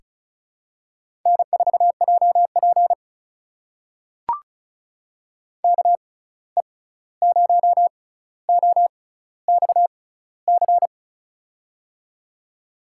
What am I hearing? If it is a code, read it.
Morse code, heard as a keyed tone, about 36 wpm, spaced as for 16 wpm, 700 Hz. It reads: N4JP KE0OXC
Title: USA Call Signs - 6x Character Spacing - 35wpm